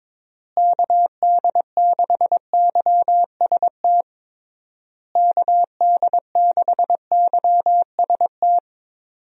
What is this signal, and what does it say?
Morse code, heard as a keyed tone, about 22 wpm, 700 Hz. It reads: KD6YST KD6YST
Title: USA Call Signs - 6 Characters - Repeated 3x - 22wpm